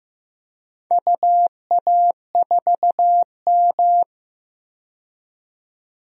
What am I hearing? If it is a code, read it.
Morse code, heard as a keyed tone, about 15 wpm, 700 Hz. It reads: UA4M
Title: DXCC-100 Call Signs - 4 Characters - 15wpm